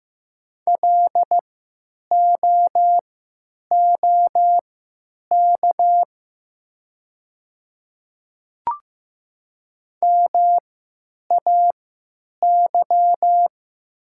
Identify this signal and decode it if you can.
Morse code, heard as a keyed tone, about 15 wpm, spaced as for 9 wpm, 700 Hz. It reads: LOOK MAY
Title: 3x Character Spacing - Top 100 Words - 15wpm